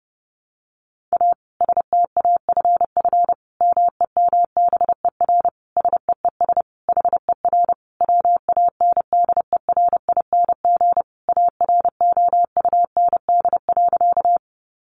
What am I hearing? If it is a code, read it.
Morse code, heard as a keyed tone, about 30 wpm, 700 Hz. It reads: A STAFF MEMBER SEES HER WANDERING AROUND.
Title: Additional Easy Phrases 30wpm